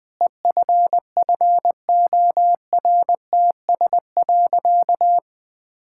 Morse code, heard as a keyed tone, about 21 wpm, 700 Hz. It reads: EFFORTS.
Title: Difficult Sentences with Hints - 20wpm